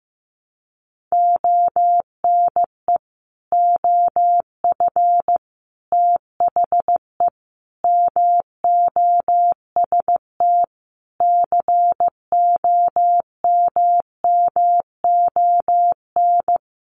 Morse code, heard as a keyed tone, about 15 wpm, 700 Hz. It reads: ONE OF THE MOST COMMON
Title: Sets of 5 Words Encore2 15wpm